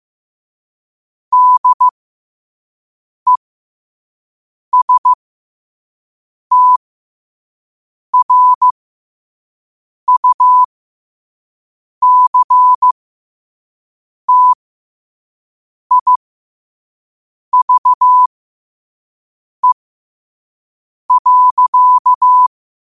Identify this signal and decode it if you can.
Morse code, heard as a keyed tone, about 15 wpm, spaced as for 5 wpm, 1000 Hz. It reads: DESTRUCTIVE.
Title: 5 WPM morse code quotes for Fri, 01 Mar 2024